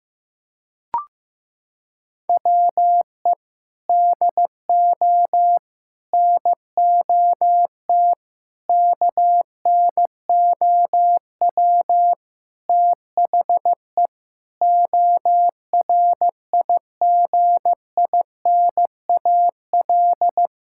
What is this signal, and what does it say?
Morse code, heard as a keyed tone, about 15 wpm, 700 Hz. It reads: WE DO NOT KNOW THE ORIGINAL
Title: Sentences from Top 1000 Words - Review Part 6 - 15wpm